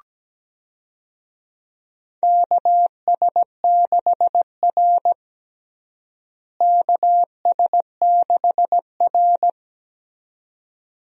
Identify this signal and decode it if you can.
Morse code, heard as a keyed tone, about 17 wpm, 700 Hz. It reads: KS6R KS6R
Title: USA Call Signs - 4 Characters - Repeated 2x - 17wpm